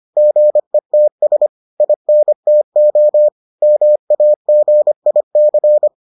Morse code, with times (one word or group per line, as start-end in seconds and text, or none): 0.2-1.5 GETS
1.8-3.3 INTO
3.6-5.9 MAGIC